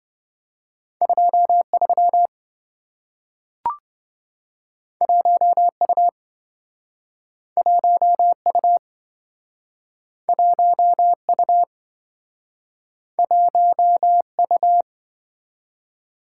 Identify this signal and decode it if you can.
Morse code, heard as a keyed tone, about 27 wpm, 700 Hz. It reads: 23 1U 1U 1U 1U